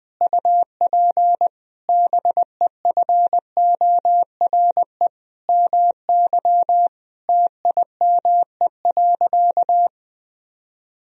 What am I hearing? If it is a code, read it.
Morse code, heard as a keyed tone, about 20 wpm, 700 Hz. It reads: UP BEFORE MY TIME.
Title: Sentences from Top 1000 Words - Review Part 1 - 20wpm